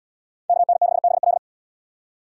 Morse code, heard as a keyed tone, about 77 wpm, 700 Hz. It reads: KN6BL